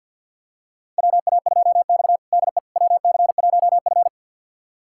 Morse code, heard as a keyed tone, about 52 wpm, 700 Hz. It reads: WA2X DE WC1F
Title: USA Callsign DE Callsign - 4 Characters - 50wpm